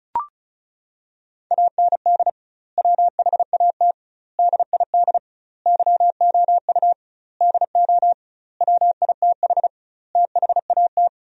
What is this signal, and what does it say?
Morse code, heard as a keyed tone, about 34 wpm, 700 Hz. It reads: AND WHAT DID YOU DO WITH THAT
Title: 2x Word-Spacing - Sentences from Top 300 Words - 35wpm